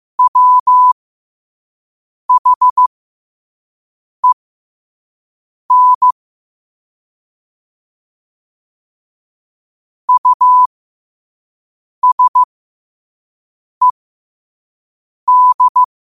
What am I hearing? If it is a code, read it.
Morse code, heard as a keyed tone, about 15 wpm, spaced as for 5 wpm, 1000 Hz. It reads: WHEN USED
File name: qotdfiles / 5wpm